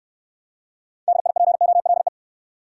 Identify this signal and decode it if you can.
Morse code, heard as a keyed tone, about 71 wpm, 700 Hz. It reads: DIPOLE